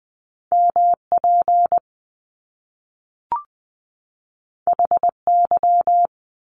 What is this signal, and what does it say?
Morse code, heard as a keyed tone, about 20 wpm, 700 Hz. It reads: MP HY
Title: Top Two Letter Combinations in Words 20wpm